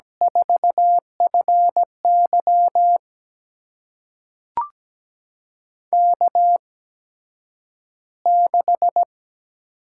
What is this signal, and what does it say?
Morse code, heard as a keyed tone, about 17 wpm, spaced as for 9 wpm, 700 Hz. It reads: 4FY K6
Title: USA Call Signs - 8x Character Spacing - 17wpm